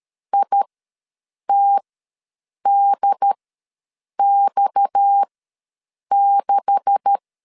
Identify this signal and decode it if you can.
Morse code, heard as a keyed tone, about 13 wpm, spaced as for 7 wpm, 800 Hz. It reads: ITDX6
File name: LEZIONI / 9 286 ws 12wpm 3351